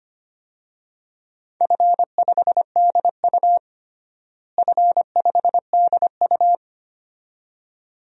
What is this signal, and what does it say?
Morse code, heard as a keyed tone, about 25 wpm, 700 Hz. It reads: F5DU F5DU
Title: DXCC-100 Call Signs - 4 Characters - Repeated 2x - 25wpm